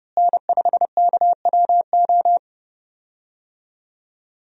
Morse code, heard as a keyed tone, about 32 wpm, 700 Hz. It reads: N5KWO
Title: USA Call Signs - 5 Characters - Repeated 3x - 30wpm